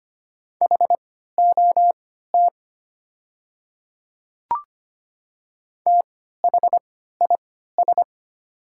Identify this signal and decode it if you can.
Morse code, heard as a keyed tone, about 25 wpm, spaced as for 14 wpm, 700 Hz. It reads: HOT THIS
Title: 3x Character Spacing - Top 100 Words - 25wpm